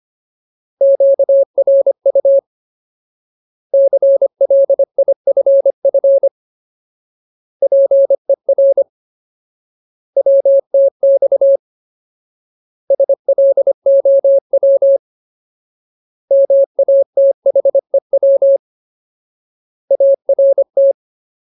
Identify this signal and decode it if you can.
Morse code, heard as a keyed tone, about 25 wpm, 550 Hz. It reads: QRU CLIFF PER WTX SLOW MATHEW ART
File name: Ultimate Word List / 25WPM Ultimate